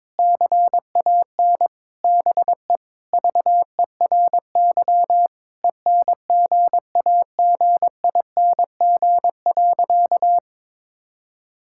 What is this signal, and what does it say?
Morse code, heard as a keyed tone, about 22 wpm, 700 Hz. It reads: CAN BE VERY ENGAGING.
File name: practice / 22wpm / Sentences with 2-Word Phrases - 22wpm